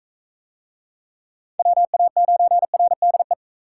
Morse code, heard as a keyed tone, about 37 wpm, 700 Hz. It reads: WA9RDE